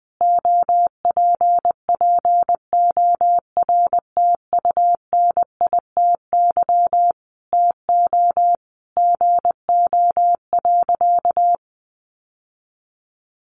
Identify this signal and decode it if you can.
Morse code, heard as a keyed tone, about 20 wpm, 700 Hz. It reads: OPPORTUNITY TO GO.
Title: Sentences from Top 1000 Words 20wpm